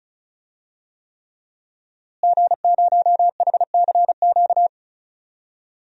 Morse code, heard as a keyed tone, about 35 wpm, 700 Hz. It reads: G0HCQ
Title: DXCC-100 Call Signs - 5 Characters - Repeated 3x - 35wpm